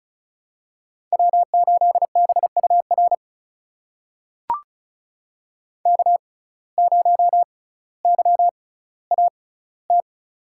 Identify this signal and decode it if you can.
Morse code, heard as a keyed tone, about 34 wpm, spaced as for 18 wpm, 700 Hz. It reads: W8BUR K0YAT
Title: USA Call Signs - 5 Characters - 6x Character Spacing - 35wpm